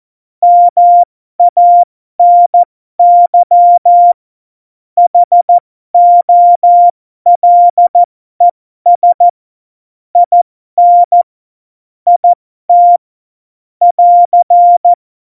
Morse code, heard as a keyed tone, about 14 wpm, spaced as for 12 wpm, 700 Hz. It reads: MANY HOLES IN IT <AR>
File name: Morsekurs / Witze